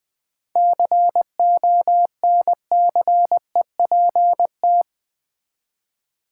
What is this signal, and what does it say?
Morse code, heard as a keyed tone, about 20 wpm, 700 Hz. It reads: CONCEPT
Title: Sentences from Top 1100 Words - New Word Drill v2 - 20wpm